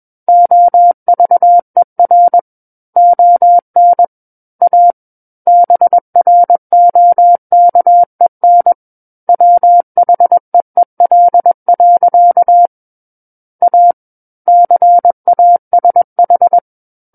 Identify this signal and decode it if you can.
Morse code, heard as a keyed tone, about 21 wpm, 700 Hz. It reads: OVER ON A BROKEN WHEEL. A CASH
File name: Book One / 12-Chapter 12 - 21 WPM